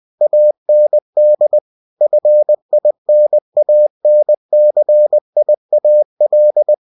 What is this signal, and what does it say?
Morse code, heard as a keyed tone, about 20 wpm, 600 Hz. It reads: AND FINANCIAL
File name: cw-books / winnie-the-pooh / winnie-20wpm-page-0120